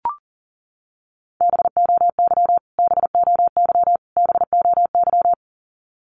Morse code, heard as a keyed tone, about 40 wpm, 700 Hz. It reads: BOY BOY BOY
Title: Top 300 Words Repeated 40wpm